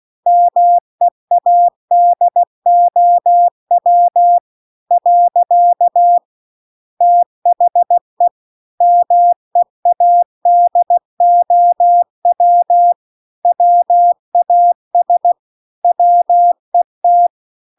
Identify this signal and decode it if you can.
Morse code, heard as a keyed tone, about 16 wpm, 700 Hz. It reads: MEADOW . THE MEADOW WAS WET
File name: 16WPM